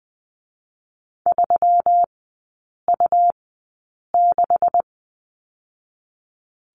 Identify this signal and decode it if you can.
Morse code, heard as a keyed tone, about 20 wpm, 690 Hz. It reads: 3 U 6
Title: 145 Lesson 24e - Letter y - Three Characters 20wpm